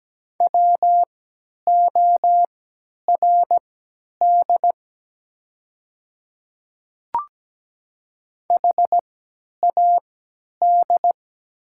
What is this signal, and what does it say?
Morse code, heard as a keyed tone, about 17 wpm, spaced as for 10 wpm, 700 Hz. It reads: WORD HAD
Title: 3x Character Spacing - Top 200 Words - 17wpm